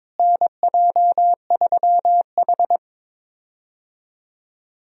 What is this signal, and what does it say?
Morse code, heard as a keyed tone, about 23 wpm, 700 Hz. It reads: NJ3H
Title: USA Call Signs - 4 Characters - Repeated 3x - 22wpm